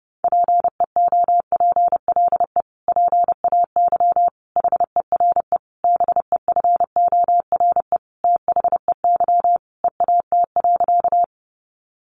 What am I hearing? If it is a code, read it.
Morse code, heard as a keyed tone, about 30 wpm, 700 Hz. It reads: PEOPLE PAY HERE BEFORE THEY EAT.
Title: Sentences from Top 700 Words 30wpm